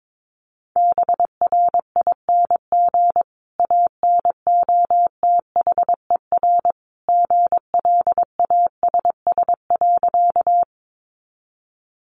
Morse code, heard as a keyed tone, about 22 wpm, 700 Hz. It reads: BRING ANOTHER GLASS.
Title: Sentences from Top 1300 Words 22wpm